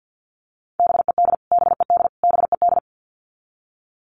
Morse code, heard as a keyed tone, about 50 wpm, 700 Hz. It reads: BED BED BED
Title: Top 500 Words Repeated 50wpm